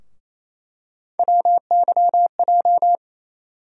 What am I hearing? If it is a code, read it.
Morse code, heard as a keyed tone, about 28 wpm, 700 Hz. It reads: WYJ
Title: USA Call Sign Postfixes - 3 Letters 28wpm